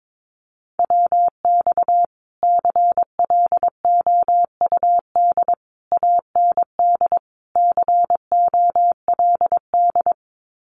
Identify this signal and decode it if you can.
Morse code, heard as a keyed tone, about 22 wpm, 700 Hz. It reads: WX CLOUD AND COLD
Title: Common QSO Phrases Encore 22wpm